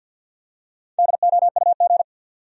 Morse code, heard as a keyed tone, about 52 wpm, 700 Hz. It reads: DOUG
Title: USA Common Names - Max 5 Letters - 50wpm